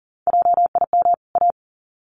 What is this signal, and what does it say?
Morse code, heard as a keyed tone, about 40 wpm, 700 Hz. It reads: JIM A
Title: CWT Exchange 40wpm